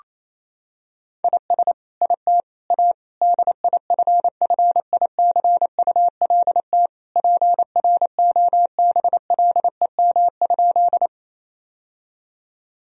Morse code, heard as a keyed tone, about 28 wpm, 700 Hz. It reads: IS IT A DIFFICULT PROBLEM?
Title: Sentences from Top 1000 Words - Review Part 2 - 28wpm